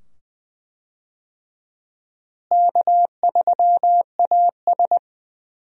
Morse code, heard as a keyed tone, about 21 wpm, 700 Hz. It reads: K3AS
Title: USA Call Signs - 4 Characters - Repeated 2x - 20wpm